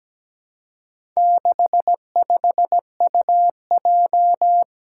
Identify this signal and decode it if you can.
Morse code, heard as a keyed tone, about 17 wpm, 700 Hz. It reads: 65UJ
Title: Four Challenging Letter-Number Combinations 17wpm